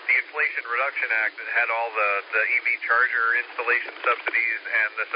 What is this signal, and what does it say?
Emergency Alert System (SAME) tones, heard as harsh buzzing bursts, with speech. It is heard from 0.9 to 4.9 s.